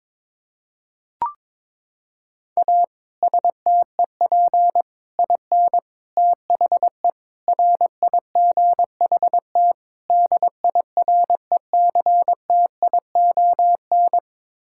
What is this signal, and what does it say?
Morse code, heard as a keyed tone, about 22 wpm, 700 Hz. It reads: A STEP IN THE RIGHT DIRECTION
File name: practice / 22wpm / Sets of 6 Words - 22wpm